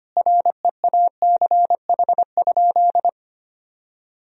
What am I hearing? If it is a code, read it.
Morse code, heard as a keyed tone, about 25 wpm, 700 Hz. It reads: REACH?
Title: Sentences from Top 1000 Words - Review Part 4 - 25wpm